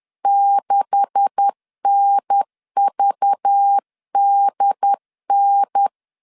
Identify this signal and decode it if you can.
Morse code, heard as a keyed tone, about 10 wpm, 800 Hz. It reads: 6NVDN
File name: CW / DET / 1 9 1500C S C 10wpm 4557